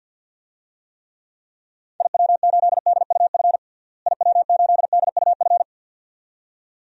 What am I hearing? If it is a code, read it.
Morse code, heard as a keyed tone, about 51 wpm, 700 Hz. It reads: IW8DUF IW8DUF
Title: DXCC-100 Call Signs - 6 Characters - Repeated 2x - 50wpm